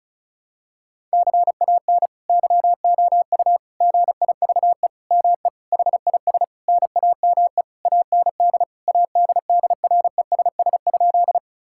Morse code, heard as a keyed tone, about 36 wpm, 700 Hz. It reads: CAN YOU GIVE ME HIS NAME AND ADDRESS?
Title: Sentences from Top 1000 Words - Review Part 6 - 35wpm